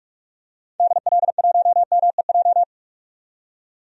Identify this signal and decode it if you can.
Morse code, heard as a keyed tone, about 46 wpm, 700 Hz. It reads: DL1MEJ
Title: DXCC-100 Call Signs - 6 Characters - 45wpm